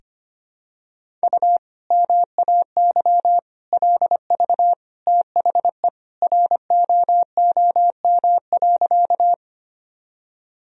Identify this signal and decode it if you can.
Morse code, heard as a keyed tone, about 25 wpm, 700 Hz. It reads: U MAY LV THE ROOM.
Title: Sentences from Top 400 Words and Abbreviations 25wpm